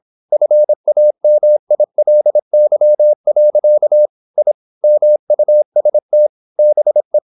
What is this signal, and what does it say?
Morse code, heard as a keyed tone, about 26 wpm, 600 Hz. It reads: FAMILY. I MUST BE
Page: Never 26wpm | CW med Gnister